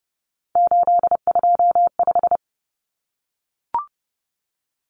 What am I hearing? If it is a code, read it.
Morse code, heard as a keyed tone, about 30 wpm, 700 Hz. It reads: 825
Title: Common Cut Numbers - 3-Digit 30wpm